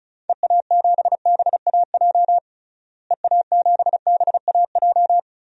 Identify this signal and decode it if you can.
Morse code, heard as a keyed tone, about 36 wpm, 700 Hz. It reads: EA7BAJ EA7BAJ
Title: DXCC-100 Call Signs - 6 Characters - Repeated 3x - 35wpm